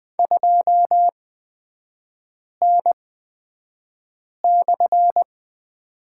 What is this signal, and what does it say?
Morse code, heard as a keyed tone, about 20 wpm, spaced as for 5 wpm, 700 Hz. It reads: 2N/
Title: 224 Lesson 36e - BT - Three Characters 20wpm